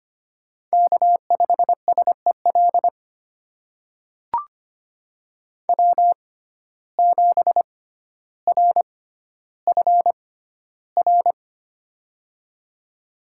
Morse code, heard as a keyed tone, about 25 wpm, spaced as for 13 wpm, 700 Hz. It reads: K5SEL W7RFR
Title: USA Call Signs - 5 Characters - 6x Character Spacing - 25wpm